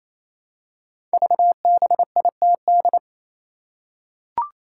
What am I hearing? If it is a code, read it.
Morse code, heard as a keyed tone, about 29 wpm, 700 Hz. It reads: VBITD
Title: Code Groups - 5 Letters - 28wpm